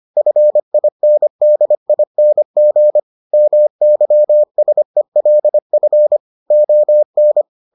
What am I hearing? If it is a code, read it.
Morse code, heard as a keyed tone, about 25 wpm, 600 Hz. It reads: FINDING MYSELF ON